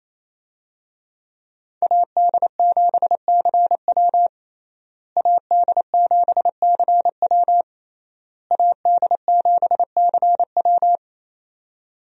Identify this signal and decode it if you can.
Morse code, heard as a keyed tone, about 28 wpm, 700 Hz. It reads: AD7CW AD7CW AD7CW
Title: USA Call Signs - 5 Characters - Repeated 3x - 28wpm